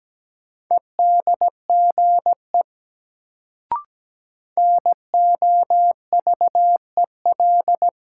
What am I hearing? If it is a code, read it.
Morse code, heard as a keyed tone, about 17 wpm, 700 Hz. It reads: EDGE NOVEL
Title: Sentences from Top 1300 Words - New Word Drill Alt - 17wpm